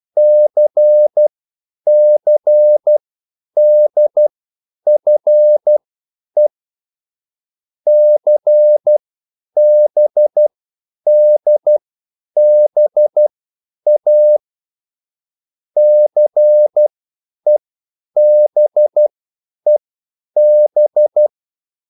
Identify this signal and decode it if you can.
Morse code, heard as a keyed tone, about 12 wpm, spaced as for 9 wpm, 600 Hz. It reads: CCDFE CBDBA CEBEB